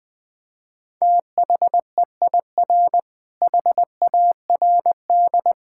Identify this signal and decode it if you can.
Morse code, heard as a keyed tone, about 21 wpm, 700 Hz. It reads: THEIR HARD
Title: Difficult Sentences with Hints - 20wpm